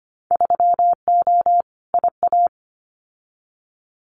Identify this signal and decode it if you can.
Morse code, heard as a keyed tone, about 25 wpm, 700 Hz. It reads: 3O IA
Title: Winter Field Day Exchanges 25wpm